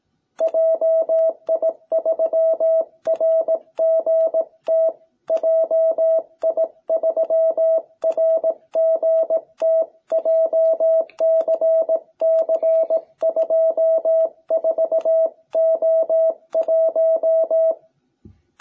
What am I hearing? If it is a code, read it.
Morse code, heard as a keyed tone, about 18 wpm, 660 Hz. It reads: JI3RGT JI3RGT JCC24O1